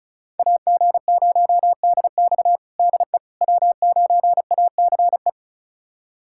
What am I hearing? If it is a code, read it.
Morse code, heard as a keyed tone, about 36 wpm, 700 Hz. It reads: AG0DX DE W9ACE
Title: USA Callsign DE Callsign - 5 Characters - 35wpm